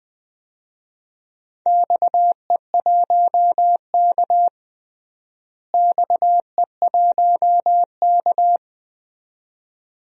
Morse code, heard as a keyed tone, about 20 wpm, 700 Hz. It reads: XE1K XE1K
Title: DXCC-100 Call Signs - 4 Characters - Repeated 2x - 20wpm